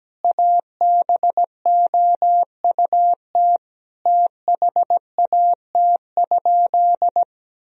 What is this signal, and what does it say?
Morse code, heard as a keyed tone, about 17 wpm, 700 Hz. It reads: ABOUT THAT?
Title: Sentences from Top 1000 Words - Review Part 4 - 17wpm